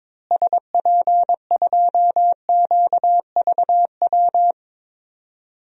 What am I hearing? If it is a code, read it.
Morse code, heard as a keyed tone, about 21 wpm, 700 Hz. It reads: SP2QVW
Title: DXCC-100 Call Signs - 6 Characters - Repeated 3x - 22wpm